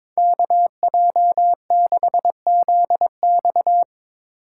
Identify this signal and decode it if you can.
Morse code, heard as a keyed tone, about 24 wpm, 700 Hz. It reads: KJ6ZX
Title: Speed Racing - USA 2x2 Call Signs 22wpm